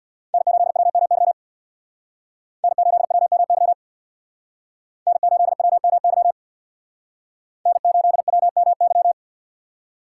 Morse code, heard as a keyed tone, about 62 wpm, 700 Hz. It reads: N8WMY N8WMY N8WMY N8WMY
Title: Speed Racing - USA 1x3 Call Signs 50wpm